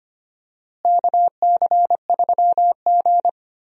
Morse code, heard as a keyed tone, about 25 wpm, 700 Hz. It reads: KC3G
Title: USA Call Signs - 4 Characters - Repeated 3x - 25wpm